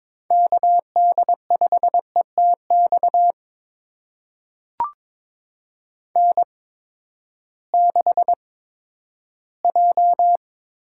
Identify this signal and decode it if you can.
Morse code, heard as a keyed tone, about 22 wpm, spaced as for 13 wpm, 700 Hz. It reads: KD5ETX N6J
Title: USA Call Signs - 8x Character Spacing - 22wpm